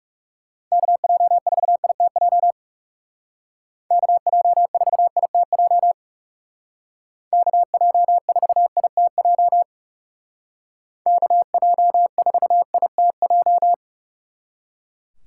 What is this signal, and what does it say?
Morse code, heard as a keyed tone, about 37 wpm, 700 Hz. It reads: KJ4ITJ KJ4ITJ KJ4ITJ KJ4ITJ
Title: Speed Racing - USA 2x3 Call Signs 30wpm